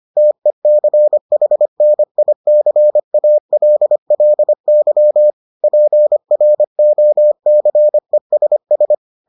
Morse code, heard as a keyed tone, about 25 wpm, 600 Hz. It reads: TECHNICALLY PROCESS